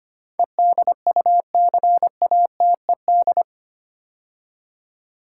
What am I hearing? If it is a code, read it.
Morse code, heard as a keyed tone, about 25 wpm, 700 Hz. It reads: EDUCATED
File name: 8 Letter Words - First 3 Letters Given - 25wpm